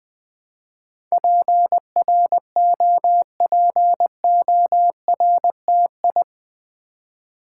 Morse code, heard as a keyed tone, about 20 wpm, 700 Hz. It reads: PROPORTI
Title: Build A Word - 4 Letters At A Time - 10-Letter Words - 20wpm